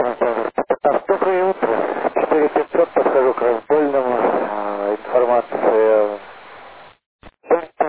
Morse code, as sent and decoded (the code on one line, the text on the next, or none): none